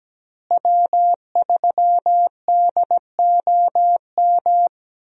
Morse code, heard as a keyed tone, about 17 wpm, 700 Hz. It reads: W3DOM
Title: USA Call Signs - 5 Characters - Repeated 2x - 17wpm